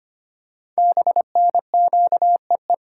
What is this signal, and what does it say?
Morse code, heard as a keyed tone, about 25 wpm, 700 Hz. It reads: BNQEE
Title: Code Groups - 5 Letters - 25wpm